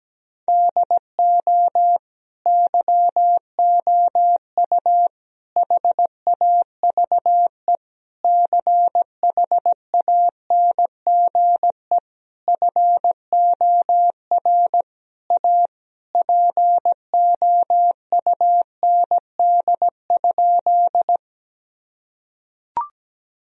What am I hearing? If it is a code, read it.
Morse code, heard as a keyed tone, about 17 wpm, 700 Hz. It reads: DO YOU HAVE CHANGE FOR A POUND?
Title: Sentences from Top 1000 Words - Review Part 2 - 17wpm